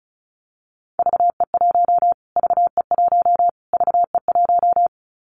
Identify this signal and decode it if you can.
Morse code, heard as a keyed tone, about 34 wpm, 700 Hz. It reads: VE1 VE1 VE1
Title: Callsign Prefixes for Top 25 DXCC - Repeated 3x - 35wpm